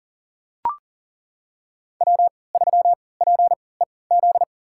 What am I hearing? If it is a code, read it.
Morse code, heard as a keyed tone, about 37 wpm, spaced as for 22 wpm, 700 Hz. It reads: W3PEZ
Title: USA Call Signs - 3x Character Spacing - 40wpm